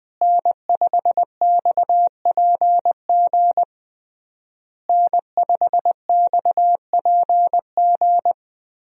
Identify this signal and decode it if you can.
Morse code, heard as a keyed tone, about 21 wpm, 700 Hz. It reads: N5XPG N5XPG
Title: USA Call Signs - 5 Characters - Repeated 3x - 20wpm